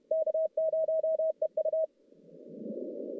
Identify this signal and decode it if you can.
Morse code, heard as a keyed tone, about 31 wpm, 640 Hz. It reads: K0EU
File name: contests / CW-SS-k5tr- -1200